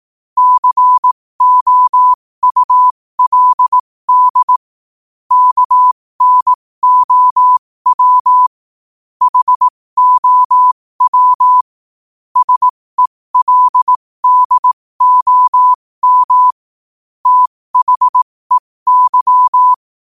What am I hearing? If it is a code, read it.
Morse code, heard as a keyed tone, about 18 wpm, spaced as for 15 wpm, 1000 Hz. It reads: COULD KNOW HOW SELDOM THEY
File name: qotdfiles / 15wpm